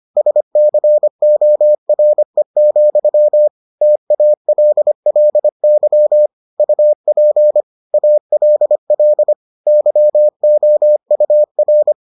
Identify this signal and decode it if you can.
Morse code, heard as a keyed tone, about 25 wpm, 600 Hz. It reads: SCORE, TALLY UP ALL YOUR